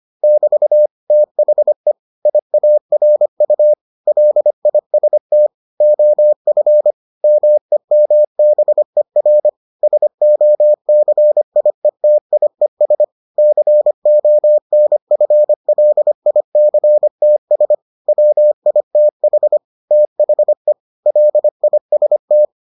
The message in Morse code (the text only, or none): <BT> THE IARU LIST OF MEMBER SOCIETIES CONFLICTS WITH THE LIST